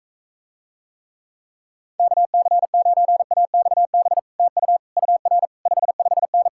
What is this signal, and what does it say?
Morse code, heard as a keyed tone, about 45 wpm, 700 Hz. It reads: KC9AXB TU UR 55N